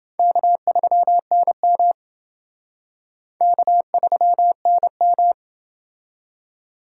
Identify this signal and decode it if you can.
Morse code, heard as a keyed tone, about 28 wpm, 690 Hz. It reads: K3NM K3NM